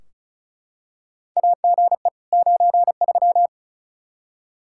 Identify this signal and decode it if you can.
Morse code, heard as a keyed tone, about 36 wpm, 700 Hz. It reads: AGE 93
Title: Practice QSO - Exchange 3 - More Info - 35wpm